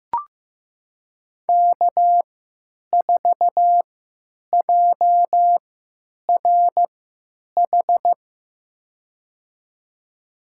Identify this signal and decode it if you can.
Morse code, heard as a keyed tone, about 15 wpm, spaced as for 9 wpm, 700 Hz. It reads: K4JRH
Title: USA Call Signs - 3x Character Spacing - 15wpm